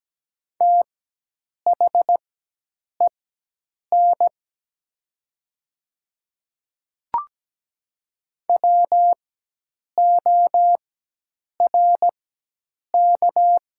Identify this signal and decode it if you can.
Morse code, heard as a keyed tone, about 17 wpm, spaced as for 8 wpm, 700 Hz. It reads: THEN WORK
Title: 4x Character Spacing - Top 200 Words - 17wpm